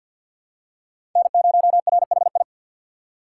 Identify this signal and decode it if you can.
Morse code, heard as a keyed tone, about 45 wpm, 700 Hz. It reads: N0RHI
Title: USA Call Signs - 5 Characters - 2x Character Spacing - 50wpm